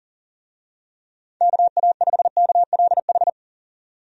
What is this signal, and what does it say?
Morse code, heard as a keyed tone, about 40 wpm, 700 Hz. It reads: KA5KLH
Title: USA Call Signs - 6 Characters - 40wpm